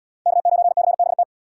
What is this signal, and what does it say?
Morse code, heard as a keyed tone, about 77 wpm, 700 Hz. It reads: K9CBT